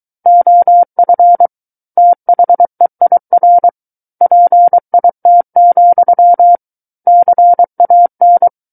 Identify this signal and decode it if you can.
Morse code, heard as a keyed tone, about 23 wpm, 700 Hz. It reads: OF THEIR PIT, CAN
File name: Book One / 14-Chapter 14 - 23 WPM